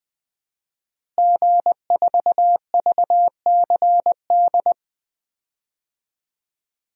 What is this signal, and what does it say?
Morse code, heard as a keyed tone, about 20 wpm, 700 Hz. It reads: G4VCD
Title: DXCC-100 Call Signs - 5 Characters - Repeated 2x - 20wpm